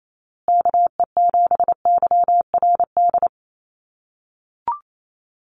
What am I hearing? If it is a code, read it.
Morse code, heard as a keyed tone, about 28 wpm, 700 Hz. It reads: KE7YRD
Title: USA Call Signs Repeated 2x 28wpm